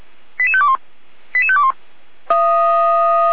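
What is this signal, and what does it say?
Five-tone selective call (ZVEI), heard as quick runs of five tones, tones of about 70 ms. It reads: 88521 88521
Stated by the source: ZVEI five-tone call received by SDR.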